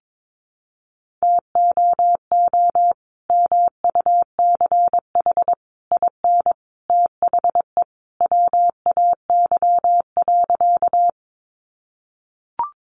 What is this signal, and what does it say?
Morse code, heard as a keyed tone, about 22 wpm, 700 Hz. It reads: TOO MUCH IN THE WAY.
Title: Sentences from Top 200 Words 22wpm